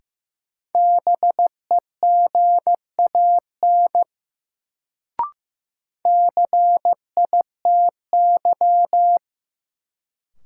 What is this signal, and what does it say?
Morse code, heard as a keyed tone, about 15 wpm, 700 Hz. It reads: BEGAN CITY
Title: Sentences from Top 300 Words - New Word Drill - 15wpm